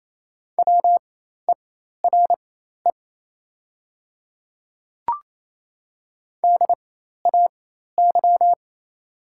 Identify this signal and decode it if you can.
Morse code, heard as a keyed tone, about 28 wpm, spaced as for 13 wpm, 700 Hz. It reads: WERE DAY
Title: 4x Character Spacing - Top 100 Words - 28wpm